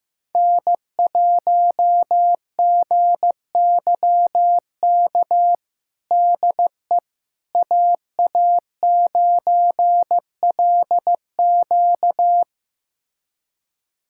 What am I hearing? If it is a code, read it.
Morse code, heard as a keyed tone, about 15 wpm, 700 Hz. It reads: N1GYK DE AA9LQ
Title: USA Callsign DE Callsign - 5 Characters - 15wpm